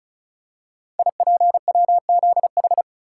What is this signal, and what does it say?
Morse code, heard as a keyed tone, about 36 wpm, 690 Hz. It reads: IPWZH
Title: Code Groups - 5 Letters - 35wpm